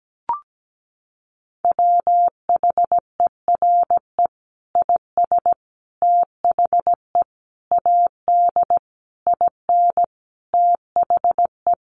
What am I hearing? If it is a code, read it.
Morse code, heard as a keyed tone, about 17 wpm, 700 Hz. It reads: WHERE IS THE AD IN THE
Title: Sentences from Top 1300 Words 17wpm